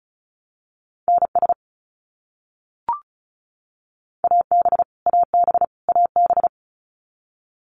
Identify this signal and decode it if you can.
Morse code, heard as a keyed tone, about 34 wpm, 700 Hz. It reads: NS AB AB AB
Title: Canadian Provinces and Territories - Repeated 3x - 35wpm